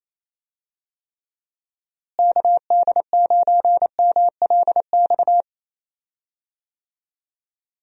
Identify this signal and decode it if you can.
Morse code, heard as a keyed tone, about 27 wpm, 700 Hz. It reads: KD9MLX
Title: USA Call Signs - 6 Characters - Repeated 3x - 28wpm